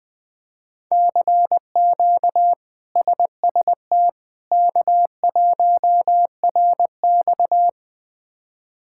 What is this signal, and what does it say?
Morse code, heard as a keyed tone, about 20 wpm, 700 Hz. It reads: CQ SST K1RX
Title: K1USN SST QSO Practice - 20wpm